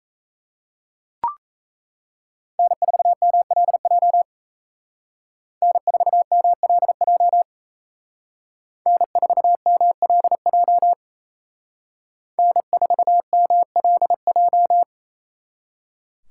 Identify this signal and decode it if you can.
Morse code, heard as a keyed tone, about 36 wpm, 700 Hz. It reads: N4MLJ N4MLJ N4MLJ N4MLJ